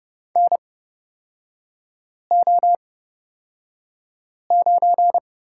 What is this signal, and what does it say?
Morse code, heard as a keyed tone, about 30 wpm, spaced as for 5 wpm, 700 Hz. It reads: NO9
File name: Lesson 37e - Letter x - Three Characters 30wpm